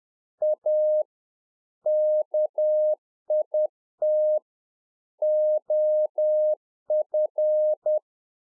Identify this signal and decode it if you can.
Morse code, heard as a keyed tone, about 10 wpm, 610 Hz. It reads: A KIT OF